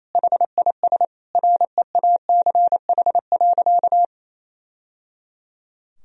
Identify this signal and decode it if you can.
Morse code, heard as a keyed tone, about 28 wpm, 700 Hz. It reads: HIS REACH.
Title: Sentences from Top 1000 Words - Review Part 2 - 28wpm